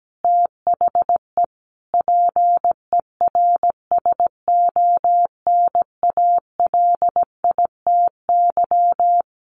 Morse code, heard as a keyed tone, about 17 wpm, 700 Hz. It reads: THE PERSONALITY
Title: Sentences from Top 1400 Words 17wpm